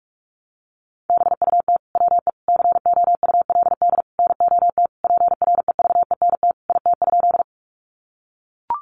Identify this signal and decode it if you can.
Morse code, heard as a keyed tone, about 44 wpm, 700 Hz. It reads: BUT WE COULD NOT PREVENT IT?
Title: Sentences from Top 1100 Words 45wpm